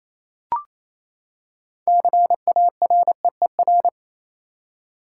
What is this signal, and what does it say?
Morse code, heard as a keyed tone, about 28 wpm, 700 Hz. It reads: CAREER
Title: Six Letter Words - 28wpm